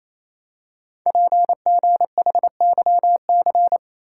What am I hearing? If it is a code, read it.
Morse code, heard as a keyed tone, about 27 wpm, 700 Hz. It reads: PGHYC
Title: Code Groups - 5 Letters - 28wpm